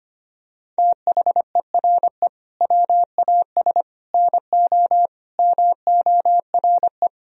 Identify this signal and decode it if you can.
Morse code, heard as a keyed tone, about 25 wpm, 700 Hz. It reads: THERE WAS NO MORE
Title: Sentences from Top 1000 Words - Review Part 4 - 25wpm